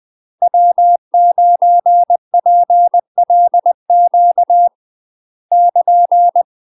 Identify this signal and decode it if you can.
Morse code, heard as a keyed tone, about 20 wpm, 700 Hz. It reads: W9PLQ <KN>
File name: UK MASTER QSOs TRAINING / 20WPM / UK MASTER QSOs 0061